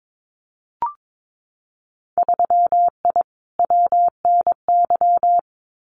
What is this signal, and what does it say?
Morse code, heard as a keyed tone, about 22 wpm, 700 Hz. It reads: 3I WNY
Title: Winter Field Day Exchanges Encore 22wpm